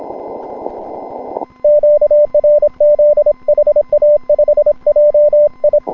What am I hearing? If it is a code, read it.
Morse code, heard as a keyed tone, about 25 wpm, 590 Hz. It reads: QRZ HA5JI
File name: n6tr / N6TR-7deHA5JIon160M-06-Sep- 31UTC